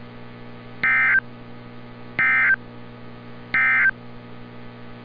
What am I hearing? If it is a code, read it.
Emergency Alert System (SAME) tones, heard as harsh buzzing bursts. It is heard from 0.7 to 4.0 s.
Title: NOAA Weather Radio - Altoona, PA